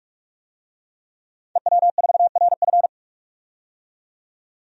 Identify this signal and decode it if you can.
Morse code, heard as a keyed tone, about 44 wpm, 700 Hz. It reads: EW4RF
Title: DXCC-100 Call Signs - 5 Characters - 45wpm